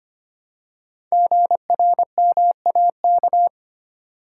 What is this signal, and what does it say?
Morse code, heard as a keyed tone, about 25 wpm, 700 Hz. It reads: GRMAK
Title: Code Groups - 5 Letters - 25wpm